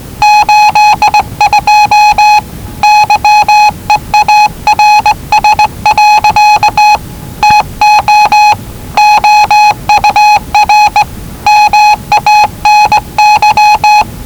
Morse code, heard as a keyed tone, about 19 wpm, 840 Hz. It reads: 82 YEARS. TO OUR MANY
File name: WellingtonRadioZlwCloseDownOnHfAndMf / 01-Zlw 4277khz 417.5 qrt until 0925z